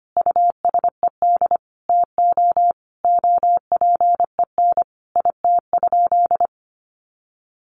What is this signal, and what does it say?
Morse code, heard as a keyed tone, about 25 wpm, 700 Hz. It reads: USED TO OPEN IT?
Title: Sentences from Top 600 Words 25wpm